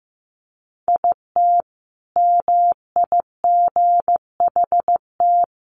Morse code, heard as a keyed tone, about 15 wpm, 700 Hz. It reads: IT MIGHT
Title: Sentences from Top 300 Words 15wpm